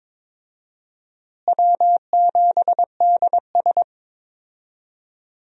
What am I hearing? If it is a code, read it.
Morse code, heard as a keyed tone, about 22 wpm, 700 Hz. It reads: W7DS
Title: USA Call Signs - 4 Characters - Repeated 3x - 22wpm